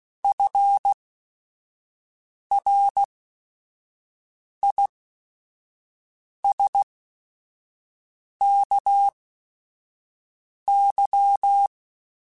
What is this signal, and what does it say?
Morse code, heard as a keyed tone, about 16 wpm, spaced as for 5 wpm, 800 Hz. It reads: FRISKY
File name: F Runs / F Runs mp3s / F Runs @5